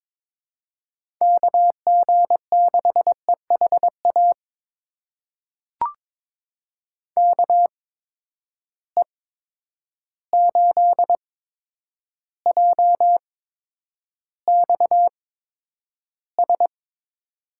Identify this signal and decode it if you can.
Morse code, heard as a keyed tone, about 21 wpm, spaced as for 9 wpm, 700 Hz. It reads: KG6EHA KE8JXS
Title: USA Call Signs - 6 Characters - 8x Character Spacing - 22wpm